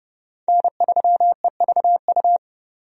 Morse code, heard as a keyed tone, about 30 wpm, 700 Hz. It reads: N3EVU
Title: USA Call Signs - 5 Characters - 30wpm